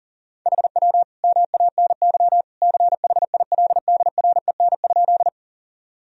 Morse code, heard as a keyed tone, about 40 wpm, 700 Hz. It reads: HW MANY CHILDREN?
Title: Sentences from Top 300 Words and Abbreviations 40wpm